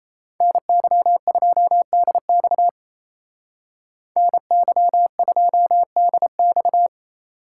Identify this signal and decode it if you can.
Morse code, heard as a keyed tone, about 37 wpm, 700 Hz. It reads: NY2DX NY2DX